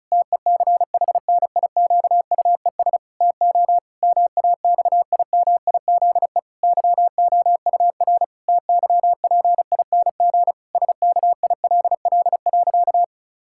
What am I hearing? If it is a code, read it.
Morse code, heard as a keyed tone, about 34 wpm, 700 Hz. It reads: TECHNIQUES TO MAXIMIZE YOUR TYPING SKILL.
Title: Pangram Challenge v2 - 35wpm